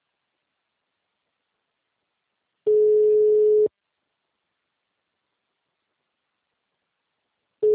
Ringback tone (Europe, 425 Hz).
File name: out- -2001- - - .35357